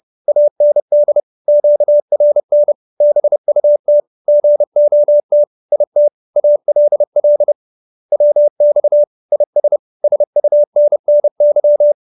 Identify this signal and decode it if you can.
Morse code, heard as a keyed tone, about 30 wpm, 590 Hz. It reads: AND QRN BUT GOT IT ALL WX IS SUNNY
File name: QSO310 30